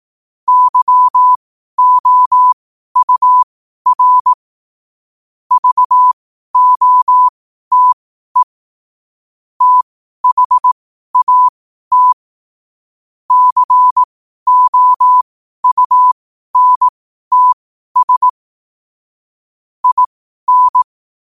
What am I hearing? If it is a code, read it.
Morse code, heard as a keyed tone, about 18 wpm, spaced as for 12 wpm, 1000 Hz. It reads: YOUR VOTE THAT COUNTS IN
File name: qotdfiles / 12wpm